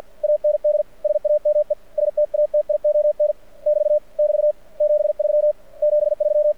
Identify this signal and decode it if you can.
Morse code, heard as a keyed tone, about 48 wpm, 600 Hz. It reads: AND SAME STATION <BT> <BT> 73 73
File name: archived / CRSnet